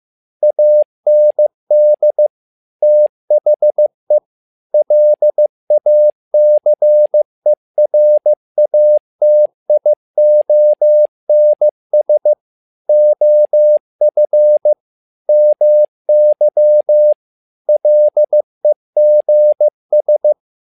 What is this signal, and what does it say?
Morse code, heard as a keyed tone, about 15 wpm, 600 Hz. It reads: AND THE LACERATIONS OF MY LEGS
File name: learning center / tower trouble / tower trouble 15wpm